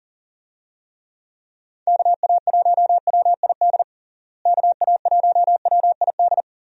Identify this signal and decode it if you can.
Morse code, heard as a keyed tone, about 40 wpm, 700 Hz. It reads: KA1WID KA1WID
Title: USA Call Signs - 6 Characters - Repeated 3x - 40wpm